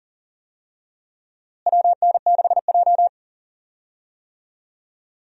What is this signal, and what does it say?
Morse code, heard as a keyed tone, about 40 wpm, 700 Hz. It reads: WN6J
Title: USA Call Signs - 4 Characters - Repeated 2x - 40wpm